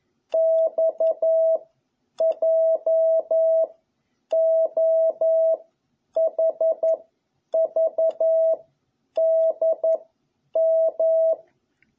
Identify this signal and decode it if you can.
Morse code, heard as a keyed tone, about 11 wpm, 650 Hz. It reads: X J O H V D M
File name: jh8cbh / 2021cw